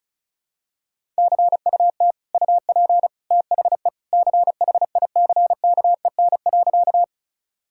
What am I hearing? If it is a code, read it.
Morse code, heard as a keyed tone, about 34 wpm, 700 Hz. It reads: CUT UP THE CHICKEN.
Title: Sentences from Top 1000 Words - Review Part 5 - 35wpm